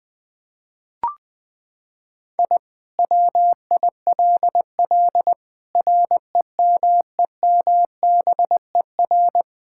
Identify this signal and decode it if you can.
Morse code, heard as a keyed tone, about 20 wpm, 700 Hz. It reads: I WILL REMEMBER
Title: Sentences with 5-Word Phrases - 20wpm